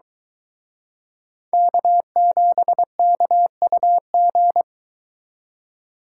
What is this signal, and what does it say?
Morse code, heard as a keyed tone, about 23 wpm, 700 Hz. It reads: K7KUG